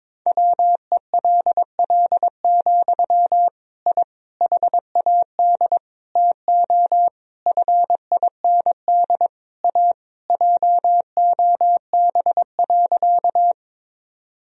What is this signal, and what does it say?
Morse code, heard as a keyed tone, about 23 wpm, 700 Hz. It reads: WELL, I HAD TO FIND A JOB.
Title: Sentences from Top 1000 Words - Review Part 3 - 22wpm